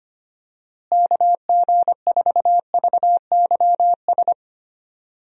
Morse code, heard as a keyed tone, about 25 wpm, 700 Hz. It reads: KG4VYS
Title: USA Call Signs - 6 Characters - Repeated 2x - 25wpm